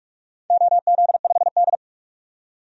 Morse code, heard as a keyed tone, about 46 wpm, 690 Hz. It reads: OZ5D
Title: DXCC-100 Call Signs - 4 Characters - 45wpm